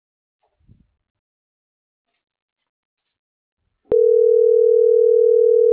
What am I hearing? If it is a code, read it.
Ringback tone (North America), 440 + 480 Hz.
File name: CENTRAL MAERSK / out nogueronmsk